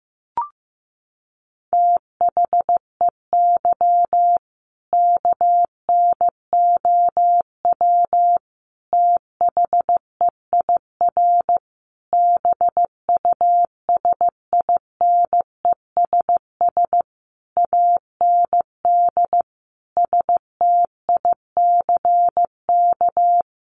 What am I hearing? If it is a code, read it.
Morse code, heard as a keyed tone, about 14 wpm, 700 Hz. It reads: THEY KNOW THEIR BUSINESS AND STICK
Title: Sentences from Top 1200 Words 15wpm